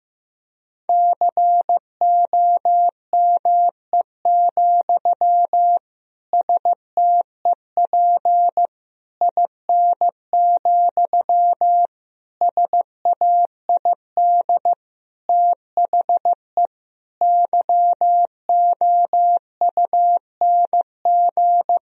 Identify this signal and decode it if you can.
Morse code, heard as a keyed tone, about 16 wpm, 700 Hz. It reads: COME, STEP IN, SAID THE YOUNG
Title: Sentences from Top 1000 Words - Review Part 2 - 15wpm